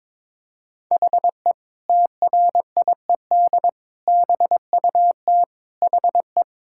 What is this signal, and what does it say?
Morse code, heard as a keyed tone, about 22 wpm, 700 Hz. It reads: HE TRIED BUT HE
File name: practice / 22wpm / Sentences with 2-Word Phrases - 22wpm